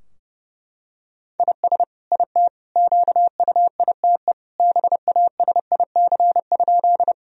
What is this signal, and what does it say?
Morse code, heard as a keyed tone, about 30 wpm, 700 Hz. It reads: IS IT QUITE BASIC?
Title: Sentences from Top 1000 Words - Review Part 3 - 30wpm